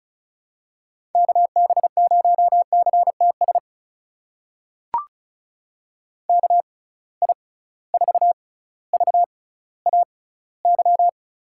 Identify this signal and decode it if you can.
Morse code, heard as a keyed tone, about 36 wpm, spaced as for 18 wpm, 700 Hz. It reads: KB0CTS KI4VAY
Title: USA Call Signs - 6 Characters - 6x Character Spacing - 35wpm